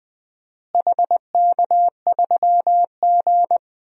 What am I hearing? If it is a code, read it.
Morse code, heard as a keyed tone, about 20 wpm, 700 Hz. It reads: HK3G
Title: DXCC-100 Call Signs - 4 Characters - Repeated 3x - 20wpm